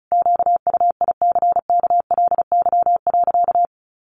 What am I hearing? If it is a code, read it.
Morse code, heard as a keyed tone, about 34 wpm, 700 Hz. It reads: QUICKLY.
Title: Sentences from Top 1300 Words 35wpm